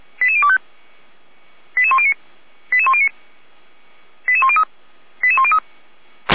Five-tone selective call (ZVEI), heard as quick runs of five tones, tones of about 70 ms. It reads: 88025 88109 88109 88103 88103
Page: /455/SDR/ZVEI/88102